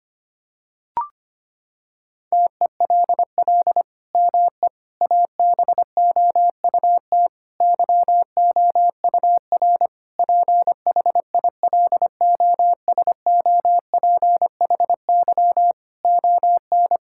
Morse code, heard as a keyed tone, about 25 wpm, 700 Hz. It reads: TELL ME ABOUT YOUR PHILOSOPHY ON
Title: Sentences from Top 1000 Words - Review Part 6 - 25wpm